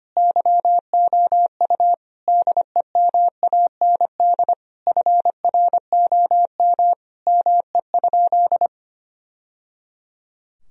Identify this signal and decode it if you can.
Morse code, heard as a keyed tone, about 25 wpm, 700 Hz. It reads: YOU DEMAND FROM ME?
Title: Sentences from Top 1000 Words - Review Part 4 - 25wpm